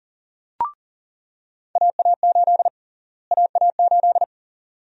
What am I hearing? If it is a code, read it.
Morse code, heard as a keyed tone, about 37 wpm, 700 Hz. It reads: AA8 AA8
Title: USA Call Sign Prefixes - Repeated 2x - 40wpm